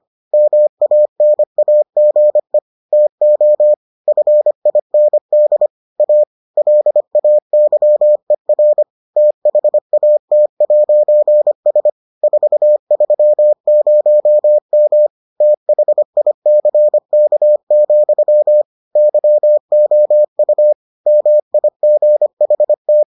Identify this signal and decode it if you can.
Morse code, heard as a keyed tone, about 25 wpm, 600 Hz. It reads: MANAGE TO FIND A LAYER THAT'S 430M THICK, YOU MIGHT